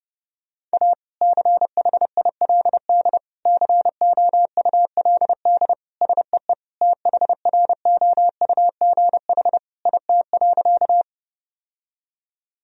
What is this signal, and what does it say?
Morse code, heard as a keyed tone, about 30 wpm, 700 Hz. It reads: A CHILD COULD SEE THROUGH IT.
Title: Sentences from Top 1000 Words - Review Part 1 - 30wpm